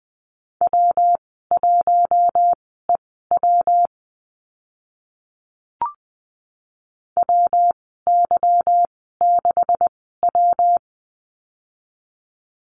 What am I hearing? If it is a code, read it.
Morse code, heard as a keyed tone, about 20 wpm, spaced as for 15 wpm, 700 Hz. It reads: W1EW WY6W
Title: 161 Lesson 26g - Letter g - Call Signs 20wpm